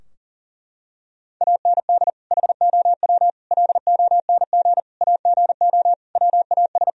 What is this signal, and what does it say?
Morse code, heard as a keyed tone, about 40 wpm, 700 Hz. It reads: AND HOW LONG AGO WAS
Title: Sentences from Top 1000 Words - Review Part 3 - 40wpm